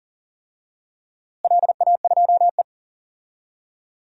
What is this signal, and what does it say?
Morse code, heard as a keyed tone, about 40 wpm, 700 Hz. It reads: LA2E
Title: DXCC-100 Call Signs - 4 Characters - Repeated 3x - 40wpm